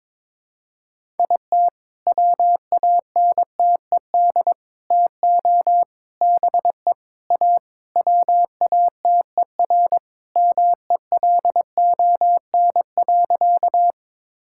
Morse code, heard as a keyed tone, about 22 wpm, 700 Hz. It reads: IT WANTED TO BE A WATERMELON.
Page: morse-code-ninja-practice-22wpm